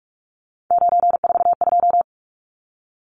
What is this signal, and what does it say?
Morse code, heard as a keyed tone, about 44 wpm, 700 Hz. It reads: 942
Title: Numbers - 3-Digit 45wpm